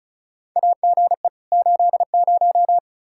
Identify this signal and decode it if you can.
Morse code, heard as a keyed tone, about 36 wpm, 700 Hz. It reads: AGE 80
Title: Practice QSO - Exchange 3 - More Info - 35wpm